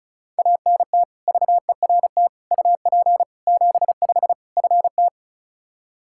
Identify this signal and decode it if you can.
Morse code, heard as a keyed tone, about 34 wpm, 700 Hz. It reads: ANT VERT UP 75 FT
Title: Practice QSO - Exchange 2 - Info - 35wpm